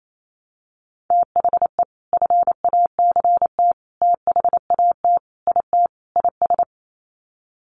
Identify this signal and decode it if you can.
Morse code, heard as a keyed tone, about 28 wpm, 700 Hz. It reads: THE FACT THAT IT IS
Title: Sets of 5 Words Encore2 28wpm